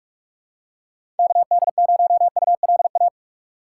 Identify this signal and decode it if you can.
Morse code, heard as a keyed tone, about 43 wpm, 700 Hz. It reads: KD0ULA
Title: USA Call Signs - 6 Characters - 2x Character Spacing - 45wpm